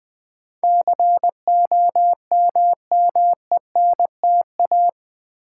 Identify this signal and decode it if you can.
Morse code, heard as a keyed tone, about 20 wpm, 700 Hz. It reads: COMMENTA
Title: Build A Word - 4 Letters At A Time - 10-Letter Words - 20wpm